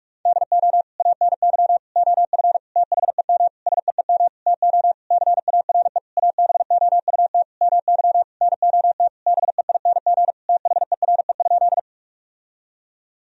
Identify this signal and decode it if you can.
Morse code, heard as a keyed tone, about 46 wpm, 700 Hz. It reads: DO ANY OF THEM SEEM TO CARE ABOUT MY NOT BEING THERE?
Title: Sentences from Top 1000 Words - Review Part 3 - 45wpm